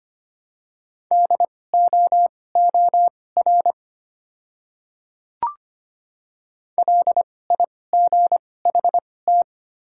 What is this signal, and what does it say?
Morse code, heard as a keyed tone, about 25 wpm, spaced as for 18 wpm, 700 Hz. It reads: DOOR LIGHT
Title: 2x Character Spacing - Top 300 Words - 25wpm